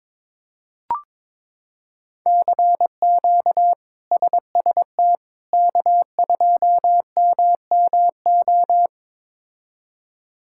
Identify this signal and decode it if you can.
Morse code, heard as a keyed tone, about 21 wpm, 700 Hz. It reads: CQ SST K2MMO
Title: K1USN SST QSO Practice - 22wpm